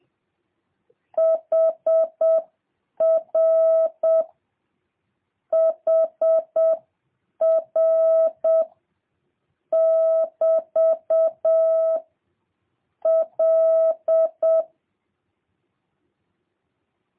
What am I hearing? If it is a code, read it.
Morse code, heard as a keyed tone, about 7 wpm, 650 Hz. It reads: HR HR <BT> L
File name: jh8cbh / 2023cw